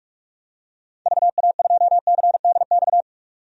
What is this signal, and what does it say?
Morse code, heard as a keyed tone, about 46 wpm, 700 Hz. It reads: VA2CDX
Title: DXCC-100 Call Signs - 6 Characters - 45wpm